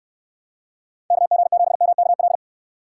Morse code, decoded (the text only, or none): KG7MKL